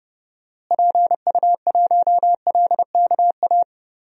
Morse code, heard as a keyed tone, about 31 wpm, 700 Hz. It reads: PU1LKA
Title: DXCC-100 Call Signs - 6 Characters - Repeated 3x - 30wpm